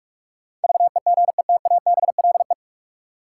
Morse code, heard as a keyed tone, about 46 wpm, 700 Hz. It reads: VEGETABLE
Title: Sentences from Top 1300 Words - New Word Drill Alt - 45wpm